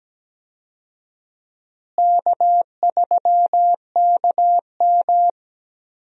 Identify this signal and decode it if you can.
Morse code, heard as a keyed tone, about 17 wpm, 700 Hz. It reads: K3KM
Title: USA Call Signs - 4 Characters - Repeated 3x - 17wpm